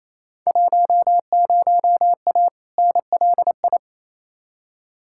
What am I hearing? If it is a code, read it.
Morse code, heard as a keyed tone, about 28 wpm, 700 Hz. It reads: 10A NLI
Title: Field Day Exchanges Encore - 28wpm